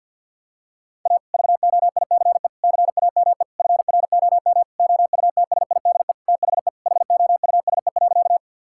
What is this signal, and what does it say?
Morse code, heard as a keyed tone, about 50 wpm, 700 Hz. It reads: A VOICE CAME FROM OUTSIDE THE HOUSE.
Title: Sentences from Top 1000 Words - Review Part 6 - 50wpm